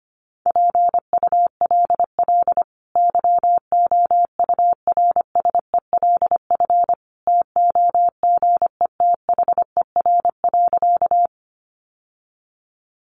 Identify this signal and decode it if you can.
Morse code, heard as a keyed tone, about 25 wpm, 700 Hz. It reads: PULL YOURSELF TOGETHER.
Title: Sentences from Top 600 Words 25wpm